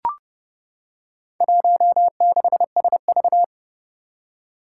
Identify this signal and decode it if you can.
Morse code, heard as a keyed tone, about 31 wpm, 700 Hz. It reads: 16SV
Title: Four Challenging Letter-Number Combinations 30wpm